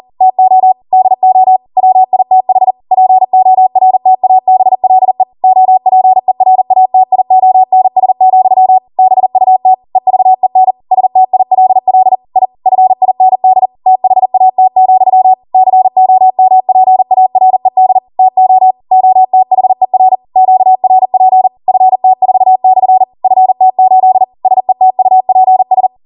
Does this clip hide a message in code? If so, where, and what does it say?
Yes: Morse code from 0.2 to 25.9 s, heard as a keyed tone, about 40 wpm, 750 Hz. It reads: TO DO WITH PORTABLE OPERATIONS, BUT EVEN STILL I FIND THAT, COMPARED TO OTHER QRP FT4/FT8 SETUPS